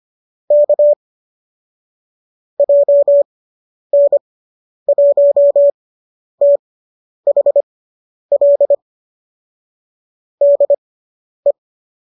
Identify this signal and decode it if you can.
Morse code, heard as a keyed tone, about 25 wpm, spaced as for 10 wpm, 590 Hz. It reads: K JN1THL DE